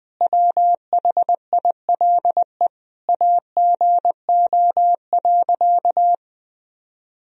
Morse code, heard as a keyed tone, about 20 wpm, 700 Hz. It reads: WHILE AGO.
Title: Sentences from Top 1000 Words - Review Part 3 - 20wpm